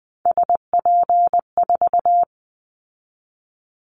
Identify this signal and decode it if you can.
Morse code, heard as a keyed tone, about 20 wpm, 700 Hz. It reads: SP4
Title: Callsign Prefixes for Top 25 DXCC - 20wpm